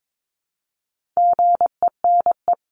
Morse code, heard as a keyed tone, about 22 wpm, 700 Hz. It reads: GENE
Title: CWT Exchange 22wpm